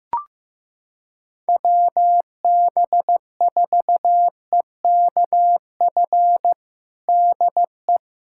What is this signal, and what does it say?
Morse code, heard as a keyed tone, about 15 wpm, 700 Hz. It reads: WB4EKF DE
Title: USA Callsign DE Callsign - 6 Characters - 15wpm